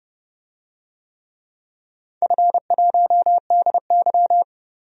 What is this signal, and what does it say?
Morse code, heard as a keyed tone, about 31 wpm, 700 Hz. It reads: F1DY
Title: DXCC-100 Call Signs - 4 Characters - 30wpm